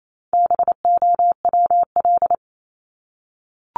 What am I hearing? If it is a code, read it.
Morse code, heard as a keyed tone, about 28 wpm, 700 Hz. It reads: BOWL
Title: Top 1001-1250 Words 28wpm